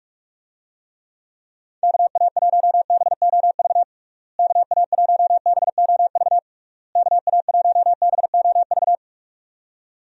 Morse code, heard as a keyed tone, about 42 wpm, 700 Hz. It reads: KA1BOV KA1BOV KA1BOV
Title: USA Call Signs - 6 Characters - Repeated 3x - 45wpm